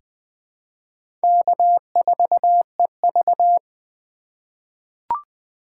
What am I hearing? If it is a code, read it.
Morse code, heard as a keyed tone, about 20 wpm, 700 Hz. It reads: K4EV
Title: USA Call Signs - 4 Characters - 3x Character Spacing - 20wpm